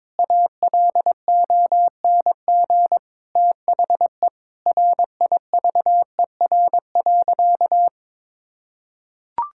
Morse code, heard as a keyed tone, about 22 wpm, 700 Hz. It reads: ALONG THE RIVER.
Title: Sentences from Top 1000 Words - Review Part 6 - 22wpm